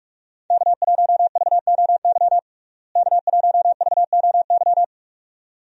Morse code, heard as a keyed tone, about 46 wpm, 700 Hz. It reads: K1VOY K1VOY
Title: USA Call Signs - 5 Characters - Repeated 2x - 45wpm